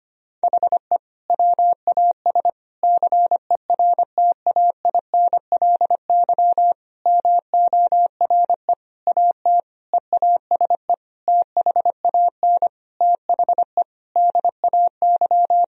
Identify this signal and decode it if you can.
Morse code, heard as a keyed tone, about 25 wpm, 700 Hz. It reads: HE WAS CERTAINLY MORE AT EASE THAN THE DAY
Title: Sentences from Top 1000 Words - Review Part 1 - 25wpm